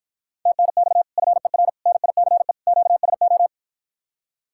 Morse code, heard as a keyed tone, about 53 wpm, 700 Hz. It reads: TNX FER NICE QSO